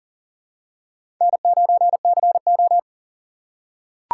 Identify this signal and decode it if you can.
Morse code, heard as a keyed tone, about 39 wpm, 700 Hz. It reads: N9CO
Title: USA Call Signs - 4 Characters - 4x Character Spacing - 40wpm